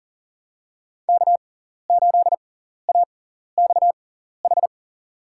Morse code, heard as a keyed tone, about 40 wpm, spaced as for 14 wpm, 700 Hz. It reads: K8AXH
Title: USA Call Signs - 5 Characters - 6x Character Spacing - 40wpm